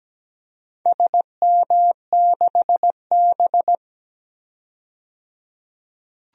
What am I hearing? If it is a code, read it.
Morse code, heard as a keyed tone, about 17 wpm, 700 Hz. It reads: SM6B
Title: DXCC-100 Call Signs - 4 Characters - 17wpm